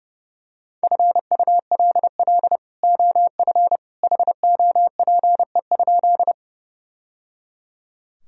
Morse code, heard as a keyed tone, about 30 wpm, 700 Hz. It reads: FULL OF HOPE?
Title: Sentences from Top 1000 Words - Review Part 3 - 30wpm